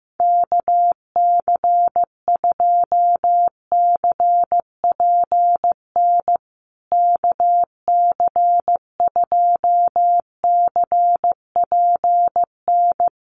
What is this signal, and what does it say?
Morse code, heard as a keyed tone, about 15 wpm, 700 Hz. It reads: KC2CPN KC2CPN
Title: USA Call Signs Repeated 3x 15wpm